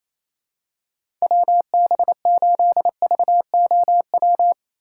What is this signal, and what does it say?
Morse code, heard as a keyed tone, about 27 wpm, 700 Hz. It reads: WB8VOW
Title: USA Call Signs - 6 Characters - 3x Character Spacing - 28wpm